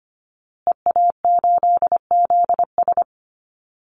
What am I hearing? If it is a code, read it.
Morse code, heard as a keyed tone, about 25 wpm, 700 Hz. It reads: EA8ZS
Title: Call Signs Repeated 2x - 25wpm